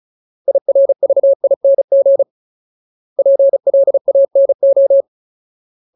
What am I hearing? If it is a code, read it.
Morse code, heard as a keyed tone, about 34 wpm, 550 Hz. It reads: IRVING PLANO
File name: Ultimate Word List / 35WPM Ultimate